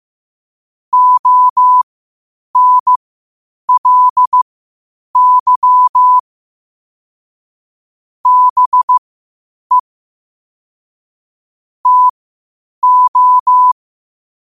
Morse code, heard as a keyed tone, about 15 wpm, spaced as for 8 wpm, 1000 Hz. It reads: ONLY BE TO
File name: qotdfiles / 8wpm